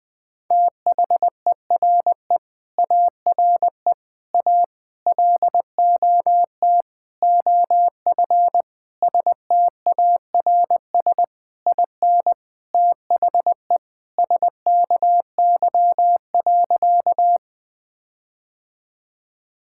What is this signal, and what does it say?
Morse code, heard as a keyed tone, about 20 wpm, 700 Hz. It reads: THERE ARE A LOT OF STARS IN THE SKY.
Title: Sentences with 4-Word Phrases - 20wpm